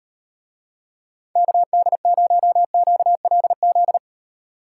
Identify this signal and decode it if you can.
Morse code, heard as a keyed tone, about 38 wpm, 700 Hz. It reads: KD0QLZ